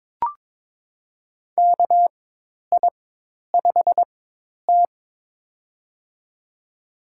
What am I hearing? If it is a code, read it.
Morse code, heard as a keyed tone, about 22 wpm, spaced as for 10 wpm, 700 Hz. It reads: KI5T
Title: USA Call Signs - 4 Characters - 4x Character Spacing - 22wpm